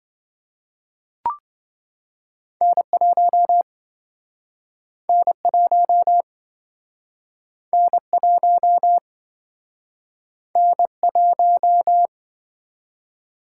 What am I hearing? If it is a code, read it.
Morse code, heard as a keyed tone, about 29 wpm, 700 Hz. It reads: N1 N1 N1 N1